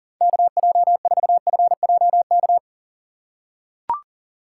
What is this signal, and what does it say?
Morse code, heard as a keyed tone, about 40 wpm, 700 Hz. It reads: KJ4FJK
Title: USA Call Signs - 6 Characters - 4x Character Spacing - 40wpm